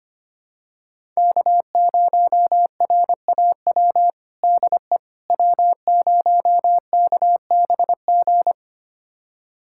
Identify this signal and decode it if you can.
Morse code, heard as a keyed tone, about 25 wpm, 700 Hz. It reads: K0RAW DE W0KBG
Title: USA Callsign DE Callsign - 5 Characters - 25wpm